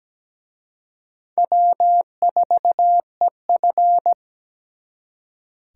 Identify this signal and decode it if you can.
Morse code, heard as a keyed tone, about 20 wpm, 700 Hz. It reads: W4EF
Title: Speed Racing - USA 1x2 Call Signs 17wpm